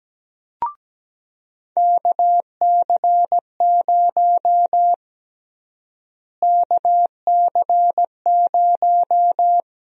Morse code, heard as a keyed tone, about 17 wpm, 700 Hz. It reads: KC0 KC0
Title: USA Call Sign Prefixes - Repeated 2x - 17wpm